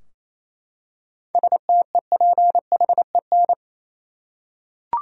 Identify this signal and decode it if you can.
Morse code, heard as a keyed tone, about 28 wpm, 700 Hz. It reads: STEPHEN
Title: USA Common Names - 28wpm